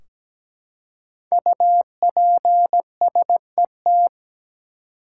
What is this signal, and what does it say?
Morse code, heard as a keyed tone, about 17 wpm, 700 Hz. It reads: UPSET
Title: Five Letter Words - 17wpm